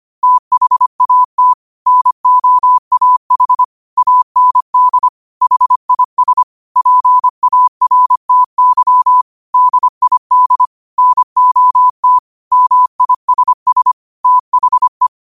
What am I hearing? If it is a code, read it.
Morse code, heard as a keyed tone, about 25 wpm, 1000 Hz. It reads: THAT NOAH AND HIS PARTY DID NOT MISS THE
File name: qotdfiles / 25wpm